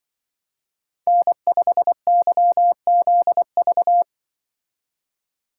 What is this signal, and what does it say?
Morse code, heard as a keyed tone, about 25 wpm, 700 Hz. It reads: N5YZV
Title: Speed Racing - USA 1x3 Call Signs 20wpm